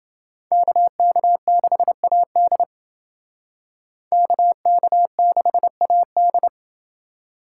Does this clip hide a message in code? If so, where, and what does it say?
Yes: Morse code from 0.5 to 6.5 s, heard as a keyed tone, about 28 wpm, 700 Hz. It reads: KK6AD KK6AD